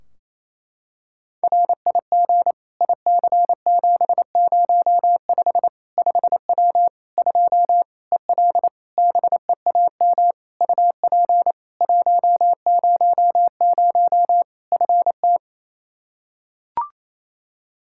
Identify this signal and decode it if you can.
Morse code, heard as a keyed tone, about 28 wpm, 700 Hz. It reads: RIG IC705 5W 2 EL BEAM UP 100 FT
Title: K1USN SST QSO Practice 28wpm